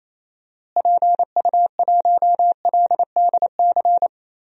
Morse code, heard as a keyed tone, about 28 wpm, 700 Hz. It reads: PU1LDC
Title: DXCC-100 Call Signs - 6 Characters - Repeated 2x - 28wpm